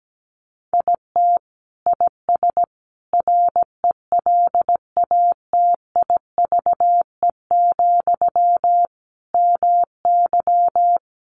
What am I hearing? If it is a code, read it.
Morse code, heard as a keyed tone, about 17 wpm, 700 Hz. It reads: IT IS RELATIVE, MY
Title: Sentences from Top 1400 Words 17wpm